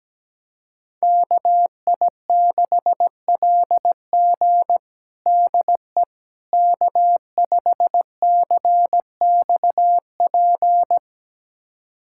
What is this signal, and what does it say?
Morse code, heard as a keyed tone, about 17 wpm, 700 Hz. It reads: KI6LG DE K5CXP
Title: USA Callsign DE Callsign - 5 Characters - 17wpm